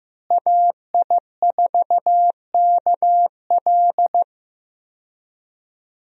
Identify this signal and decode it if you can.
Morse code, heard as a keyed tone, about 15 wpm, 700 Hz. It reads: AI4KL
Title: USA Call Signs - 5 Characters - Repeated 2x - 15wpm